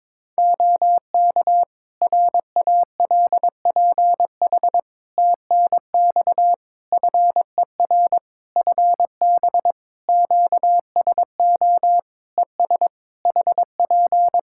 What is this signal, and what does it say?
Morse code, heard as a keyed tone, about 22 wpm, 700 Hz. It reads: OK RALPH TNX FER FB QSO ES HP
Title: Practice QSO - Exchange 4 - Ending - 22wpm